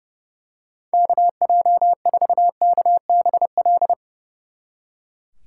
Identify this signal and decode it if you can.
Morse code, heard as a keyed tone, about 29 wpm, 700 Hz. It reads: KJ4KBL
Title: Speed Racing - USA 2x3 Call Signs 30wpm